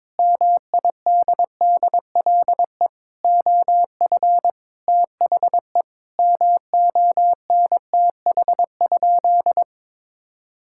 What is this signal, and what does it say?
Morse code, heard as a keyed tone, about 22 wpm, 700 Hz. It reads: MIDDLE OF THE MONTH?
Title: Sentences from Top 1000 Words - Review Part 6 - 22wpm